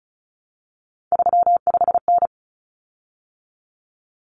Morse code, heard as a keyed tone, about 34 wpm, 700 Hz. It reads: 35N
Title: Common Cut Numbers - 3-Digit 35wpm